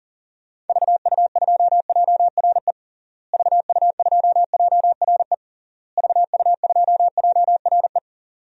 Morse code, heard as a keyed tone, about 40 wpm, 700 Hz. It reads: VU2JRE VU2JRE VU2JRE
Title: DXCC-100 Call Signs - 6 Characters - Repeated 3x - 40wpm